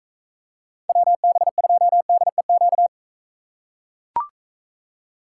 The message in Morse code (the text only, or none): WB2DEQ